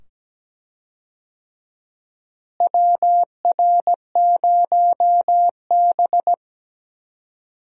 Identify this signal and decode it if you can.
Morse code, heard as a keyed tone, about 17 wpm, 700 Hz. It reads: WR0B
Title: USA Call Signs - 4 Characters - Repeated 2x - 17wpm